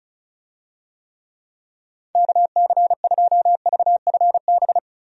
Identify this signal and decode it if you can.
Morse code, heard as a keyed tone, about 36 wpm, 700 Hz. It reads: KC2VFB
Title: USA Call Signs - 6 Characters - Repeated 2x - 35wpm